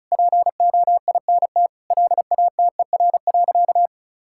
Morse code, heard as a keyed tone, about 36 wpm, 690 Hz. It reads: POINT LATER.
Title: Sentences from Top 1000 Words - Review Part 6 - 35wpm